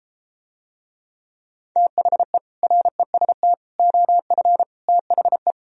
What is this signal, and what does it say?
Morse code, heard as a keyed tone, about 33 wpm, 700 Hz. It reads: THE REST OF THE